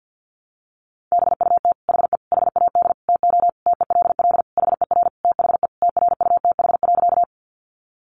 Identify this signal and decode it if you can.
Morse code, heard as a keyed tone, about 50 wpm, 700 Hz. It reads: BUT HE HAD TO TELL HER THE TRUTH.
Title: Sentences from Top 800 Words 50wpm